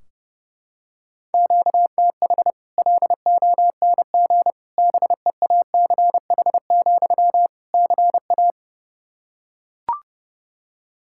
Morse code, heard as a keyed tone, about 30 wpm, 700 Hz. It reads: QTH LONG BEACH, CA
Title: Practice QSO - Exchange 1 - Beginning - 30wpm